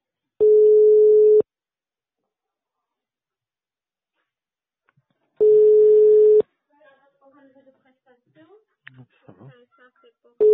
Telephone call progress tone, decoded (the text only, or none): ringback tone